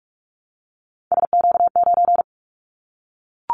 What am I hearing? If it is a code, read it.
Morse code, heard as a keyed tone, about 44 wpm, 700 Hz. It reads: SQ9
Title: Callsign Prefixes for Top 25 DXCC - Repeated 3x - 45wpm